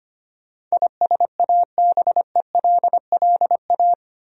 Morse code, heard as a keyed tone, about 25 wpm, 700 Hz. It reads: ISABELLA
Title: USA Common Names - 25wpm